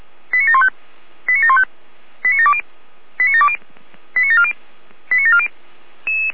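Five-tone selective call (ZVEI), heard as quick runs of five tones, tones of about 70 ms. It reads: 89715 89715 89720 89720 89740 89740
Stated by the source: ZVEI five-tone selective call received on an SDR.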